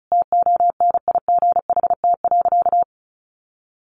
Morse code, heard as a keyed tone, about 34 wpm, 700 Hz. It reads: TONIGHT.
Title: Sentences from Top 1300 Words 35wpm